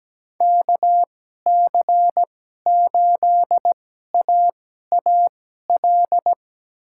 Morse code, heard as a keyed tone, about 17 wpm, spaced as for 12 wpm, 700 Hz. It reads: KC8AAL
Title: USA Call Signs - 6 Characters - 2x Character Spacing - 17wpm